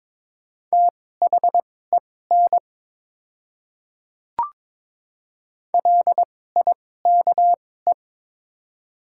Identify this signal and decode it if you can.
Morse code, heard as a keyed tone, about 22 wpm, spaced as for 16 wpm, 700 Hz. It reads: THEN LIKE
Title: 2x Character Spacing - Top 100 Words - 22wpm